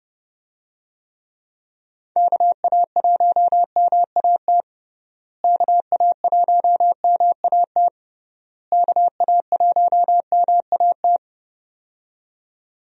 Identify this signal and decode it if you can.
Morse code, heard as a keyed tone, about 30 wpm, 700 Hz. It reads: KA1MAT KA1MAT KA1MAT
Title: USA Call Signs - 6 Characters - Repeated 3x - 30wpm